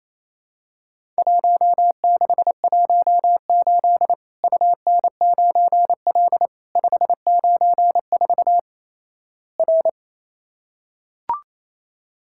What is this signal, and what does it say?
Morse code, heard as a keyed tone, about 28 wpm, 700 Hz. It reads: 1618 UN9L 594 R
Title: Worked All Europe DX Contest - QTC Exchange - 28wpm